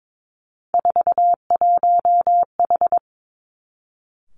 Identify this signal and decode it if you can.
Morse code, heard as a keyed tone, about 22 wpm, 700 Hz. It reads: 41H
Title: Three Challenging Letter-Number Combinations 22wpm